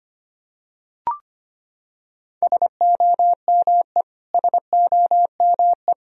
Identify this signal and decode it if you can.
Morse code, heard as a keyed tone, about 25 wpm, 700 Hz. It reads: SOME SOME
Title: Top 100 Words Repeated 25wpm